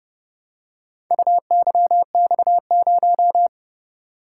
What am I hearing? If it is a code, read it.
Morse code, heard as a keyed tone, about 29 wpm, 700 Hz. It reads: UYX0